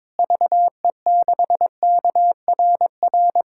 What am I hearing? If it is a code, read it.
Morse code, heard as a keyed tone, about 22 wpm, 700 Hz. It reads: VE6KRR
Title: K1USN SST QSO Practice - 22wpm